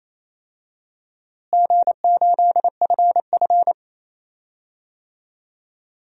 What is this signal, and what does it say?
Morse code, heard as a keyed tone, about 28 wpm, 700 Hz. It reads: G8FF
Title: DXCC-100 Call Signs - 4 Characters - Repeated 2x - 28wpm